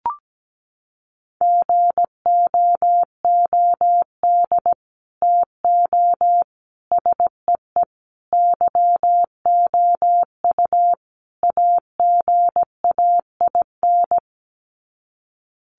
Morse code, heard as a keyed tone, about 17 wpm, 700 Hz. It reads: GOOD TO SEE YOU AGAIN
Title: Sets of 5 Words Encore2 17wpm